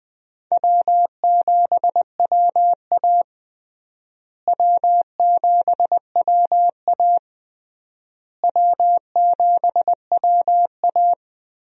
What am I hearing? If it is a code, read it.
Morse code, heard as a keyed tone, about 20 wpm, 700 Hz. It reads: W7WA W7WA W7WA
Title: USA Call Signs - 4 Characters - Repeated 3x - 20wpm